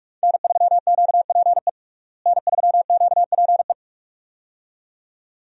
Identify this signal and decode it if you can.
Morse code, heard as a keyed tone, about 46 wpm, 700 Hz. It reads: N3QPE N3QPE
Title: USA Call Signs - 5 Characters - Repeated 2x - 45wpm